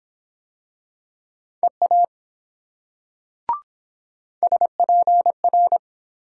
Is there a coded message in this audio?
Yes: Morse code from 1.6 to 5.8 s, heard as a keyed tone, about 22 wpm, 700 Hz. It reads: EA SPR